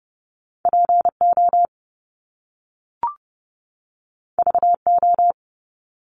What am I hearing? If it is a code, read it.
Morse code, heard as a keyed tone, about 30 wpm, 700 Hz. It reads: PO VO
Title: Top Two Letter Combinations in Words 30wpm